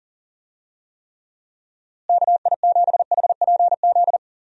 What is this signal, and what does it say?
Morse code, heard as a keyed tone, about 40 wpm, 700 Hz. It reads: KI7HPZ
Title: USA Call Signs - 6 Characters - Repeated 2x - 40wpm